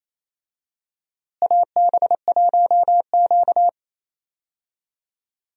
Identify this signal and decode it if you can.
Morse code, heard as a keyed tone, about 27 wpm, 700 Hz. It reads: AB1Q
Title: USA Call Signs - 4 Characters - Repeated 3x - 28wpm